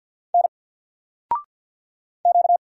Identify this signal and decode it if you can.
Morse code, heard as a keyed tone, about 50 wpm, 700 Hz. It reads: N Q
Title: Single Character (Letter:Number:Punctuation:Prosign) 50wpm